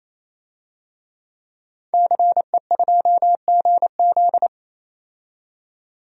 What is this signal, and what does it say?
Morse code, heard as a keyed tone, about 28 wpm, 700 Hz. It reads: CE2GZ
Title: DXCC-100 Call Signs - 5 Characters - 28wpm